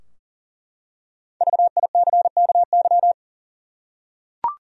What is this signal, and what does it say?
Morse code, heard as a keyed tone, about 40 wpm, 700 Hz. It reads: VICKY
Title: USA Common Names - 40wpm